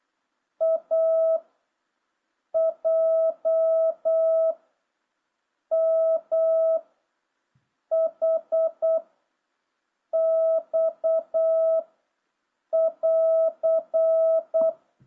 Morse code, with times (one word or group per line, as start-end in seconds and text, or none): 0.6-1.4 A
2.5-4.5 J
5.7-6.8 M
7.9-9.0 H
10.1-11.8 X
12.7-14.7 <AR>